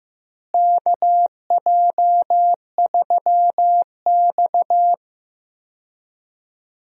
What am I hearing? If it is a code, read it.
Morse code, heard as a keyed tone, about 15 wpm, 700 Hz. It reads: KJ3X
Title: USA Call Signs - 4 Characters - Repeated 2x - 15wpm